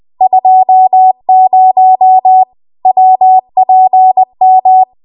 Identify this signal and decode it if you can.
Morse code, heard as a keyed tone, about 20 wpm, 750 Hz. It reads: 20 WPM